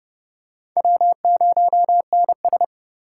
Morse code, heard as a keyed tone, about 30 wpm, 700 Hz. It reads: W0NS